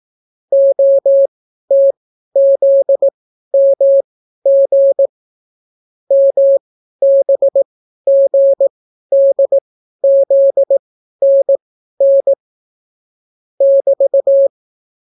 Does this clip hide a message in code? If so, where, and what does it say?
Yes: Morse code from 0.5 to 14.5 s, heard as a keyed tone, about 18 wpm, spaced as for 12 wpm, 550 Hz. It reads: OTZMG MBGDZNN <BT>